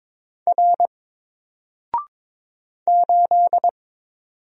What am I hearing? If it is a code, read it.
Morse code, heard as a keyed tone, about 22 wpm, 700 Hz. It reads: R 8
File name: Single-Letter-Number-22wpm